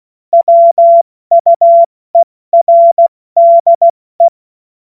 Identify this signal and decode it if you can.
Morse code, heard as a keyed tone, about 16 wpm, 680 Hz. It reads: WUERDE
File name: Morsekurs / Witze / Witz 13. April 2021, 70 F80